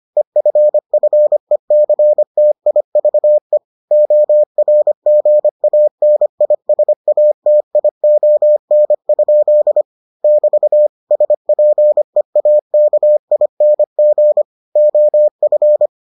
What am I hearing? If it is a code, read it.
Morse code, heard as a keyed tone, about 25 wpm, 600 Hz. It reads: EFFECTIVE ORGANISATION? <BT> SPEAKING OF